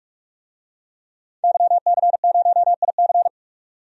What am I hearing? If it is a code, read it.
Morse code, heard as a keyed tone, about 43 wpm, 700 Hz. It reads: YC0IC
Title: DXCC-100 Call Signs - 5 Characters - 45wpm